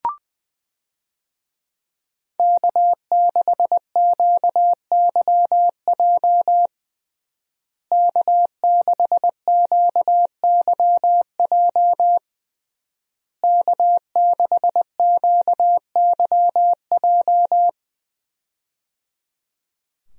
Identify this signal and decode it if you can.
Morse code, heard as a keyed tone, about 20 wpm, 700 Hz. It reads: K6QYJ K6QYJ K6QYJ
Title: USA Call Signs - 5 Characters - Repeated 3x - 20wpm